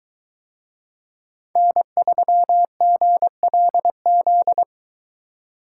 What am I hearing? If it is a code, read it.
Morse code, heard as a keyed tone, about 21 wpm, 700 Hz. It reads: N3GLZ